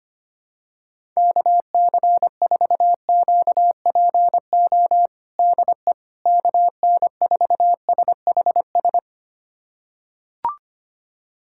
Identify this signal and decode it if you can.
Morse code, heard as a keyed tone, about 25 wpm, 700 Hz. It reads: KC4QPO DE KN4SHS
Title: USA Callsign DE Callsign 25wpm